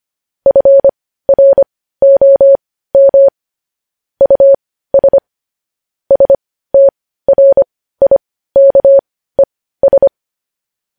Morse code, heard as a keyed tone, about 25 wpm, spaced as for 15 wpm, 550 Hz. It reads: FROM US STRIKES
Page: news headlines in morse code at 15 wpm